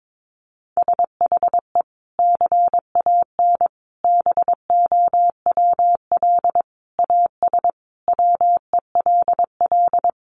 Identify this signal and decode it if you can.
Morse code, heard as a keyed tone, about 22 wpm, 700 Hz. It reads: SHE CAN BOWL AS WELL
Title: Sentences from Top 1500 Words 22wpm